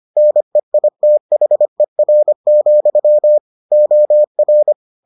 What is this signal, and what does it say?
Morse code, heard as a keyed tone, about 25 wpm, 600 Hz. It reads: NEITHER, OR